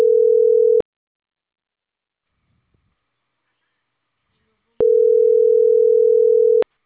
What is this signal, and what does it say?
Ringback tone (North America), 440 + 480 Hz.